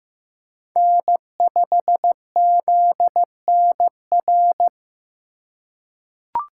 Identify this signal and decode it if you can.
Morse code, heard as a keyed tone, about 15 wpm, 700 Hz. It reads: N5ZNR
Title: USA Call Signs - 5 Characters - 4x Character Spacing - 15wpm